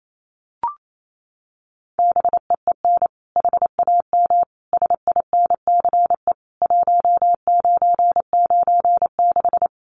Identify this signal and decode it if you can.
Morse code, heard as a keyed tone, about 28 wpm, 700 Hz. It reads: BEEN HAM SINCE 1996
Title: Common QSO Phrases 28wpm